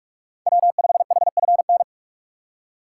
Morse code, heard as a keyed tone, about 44 wpm, 690 Hz. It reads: W5HFN
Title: USA Call Signs - 5 Characters - 2x Character Spacing - 45wpm